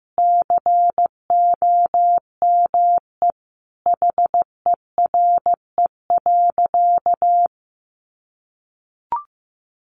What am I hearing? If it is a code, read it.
Morse code, heard as a keyed tone, about 15 wpm, 700 Hz. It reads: COME HERE.
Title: Sentences from Top 300 Words 15wpm